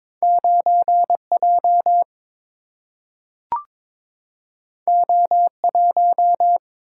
Morse code, heard as a keyed tone, about 22 wpm, 700 Hz. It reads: 9 J O 1
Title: Single-Letter Single-Number 22wpm